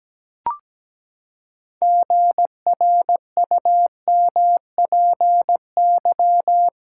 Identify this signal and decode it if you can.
Morse code, heard as a keyed tone, about 17 wpm, 700 Hz. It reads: GRUMPY
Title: Pangram Challenge v2 - 17wpm